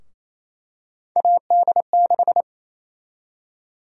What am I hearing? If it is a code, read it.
Morse code, heard as a keyed tone, about 28 wpm, 700 Hz. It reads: AD6
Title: USA Call Sign Prefixes - 28wpm